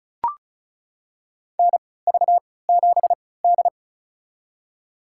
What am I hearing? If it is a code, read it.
Morse code, heard as a keyed tone, about 34 wpm, spaced as for 20 wpm, 690 Hz. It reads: NV7D
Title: USA Call Signs - 3x Character Spacing - 35wpm